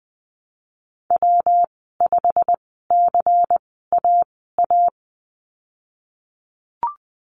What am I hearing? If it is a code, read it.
Morse code, heard as a keyed tone, about 20 wpm, spaced as for 15 wpm, 700 Hz. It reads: W5CAA
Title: 128 Lesson 21g - Number 6 - Call Signs 20wpm